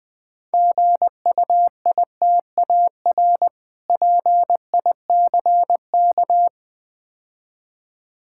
Morse code, heard as a keyed tone, about 20 wpm, 700 Hz. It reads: GUITAR PICK
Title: Easy Words - Speed-Racing with Hints - 20wpm